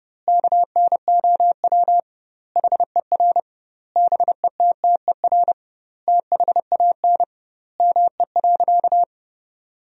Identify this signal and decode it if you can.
Morse code, heard as a keyed tone, about 32 wpm, 700 Hz. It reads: KNOW HER BETTER THAN ME.
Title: 2x Word-Spacing - Sentences from Top 400 Words - 30wpm